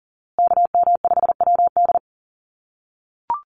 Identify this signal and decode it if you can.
Morse code, heard as a keyed tone, about 40 wpm, 700 Hz. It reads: KM5WD
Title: USA Call Signs Repeated 2x 40wpm